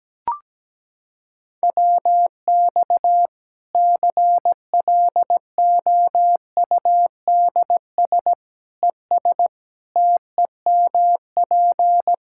Morse code, heard as a keyed tone, about 17 wpm, 700 Hz. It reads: WX CLOUDS ES TEMP
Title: Practice QSO - Exchange 2 - Info - 17wpm